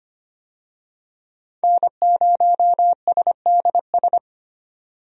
Morse code, heard as a keyed tone, about 25 wpm, 700 Hz. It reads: N0SDS
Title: USA Call Signs - 5 Characters - 25wpm